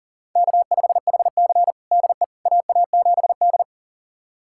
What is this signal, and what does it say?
Morse code, heard as a keyed tone, about 40 wpm, 700 Hz. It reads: K5HC DE AA7D
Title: USA Callsign DE Callsign - 4 Characters - 40wpm